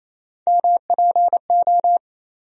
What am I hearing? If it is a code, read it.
Morse code, heard as a keyed tone, about 28 wpm, 700 Hz. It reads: MPO
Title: USA Call Sign Postfixes - 3 Letters 28wpm